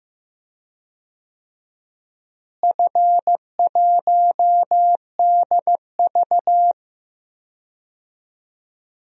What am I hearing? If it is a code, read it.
Morse code, heard as a keyed tone, about 15 wpm, 700 Hz. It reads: F1DV
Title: DXCC-100 Call Signs - 4 Characters - Repeated 2x - 15wpm